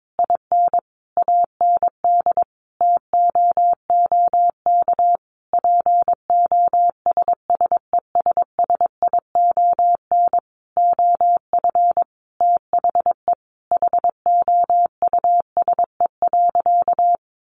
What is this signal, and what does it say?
Morse code, heard as a keyed tone, about 22 wpm, 700 Hz. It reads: IN AND TOOK POSSESSION OF THE HOUSE.
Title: Sentences from Top 1400 Words 22wpm